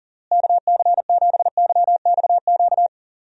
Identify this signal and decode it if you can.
Morse code, heard as a keyed tone, about 40 wpm, 700 Hz. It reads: KC7YXQ
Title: USA Call Signs - 6 Characters - Repeated 3x - 40wpm